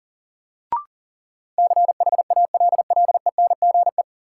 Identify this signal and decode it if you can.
Morse code, heard as a keyed tone, about 40 wpm, 700 Hz. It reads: CHALLENGE
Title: Sentences from Top 1000 Words - Review Part 7 - New Word Drill - 40wpm